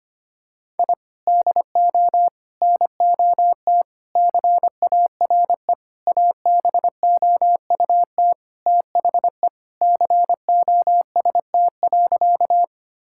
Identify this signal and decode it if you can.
Morse code, heard as a keyed tone, about 25 wpm, 700 Hz. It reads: I DO NOT CARE ABOUT THE COST.
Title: Sentences from Top 1000 Words - Review Part 5 - 25wpm